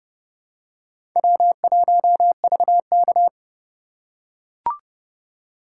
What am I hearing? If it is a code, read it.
Morse code, heard as a keyed tone, about 29 wpm, 700 Hz. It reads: W1VK